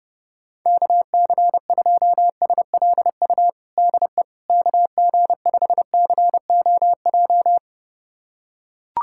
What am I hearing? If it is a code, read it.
Morse code, heard as a keyed tone, about 30 wpm, 700 Hz. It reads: KC2SLU DE KG5COJ
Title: USA Callsign DE Callsign - 6 Characters - 30wpm